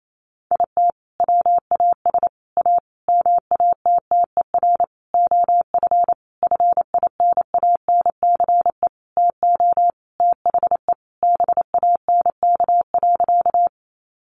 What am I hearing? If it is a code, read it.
Morse code, heard as a keyed tone, about 28 wpm, 700 Hz. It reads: IT WAS A MATTER OF FINANCE TO THE BANK.
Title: Sentences from Top 1500 Words 28wpm